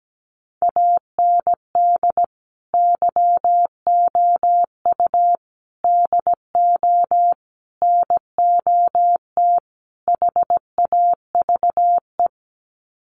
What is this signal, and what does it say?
Morse code, heard as a keyed tone, about 17 wpm, 690 Hz. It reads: AND YOU DO NOT HAVE
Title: Sets of 5 Words Encore 17wpm